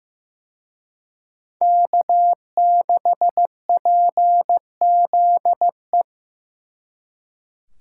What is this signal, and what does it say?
Morse code, heard as a keyed tone, about 18 wpm, 700 Hz. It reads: K6PZE
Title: Speed Racing - USA 1x3 Call Signs 15wpm